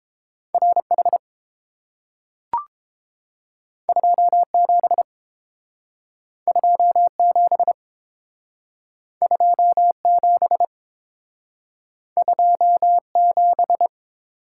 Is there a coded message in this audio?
Yes: Morse code from 0.5 to 13.9 s, heard as a keyed tone, about 30 wpm, 700 Hz. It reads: RH 27 27 27 27